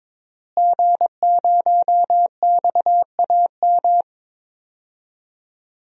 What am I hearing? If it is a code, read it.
Morse code, heard as a keyed tone, about 22 wpm, 700 Hz. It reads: G0XAM
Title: DXCC-100 Call Signs - 5 Characters - Repeated 3x - 22wpm